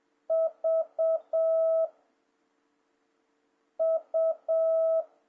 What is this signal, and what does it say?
Morse code, heard as a keyed tone, about 7 wpm, 650 Hz. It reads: V U
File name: jh8cbh / 2026cw